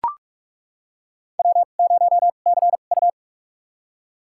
Morse code, heard as a keyed tone, about 43 wpm, spaced as for 32 wpm, 700 Hz. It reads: W0CU
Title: USA Call Signs - 4 Characters - 2x Character Spacing - 45wpm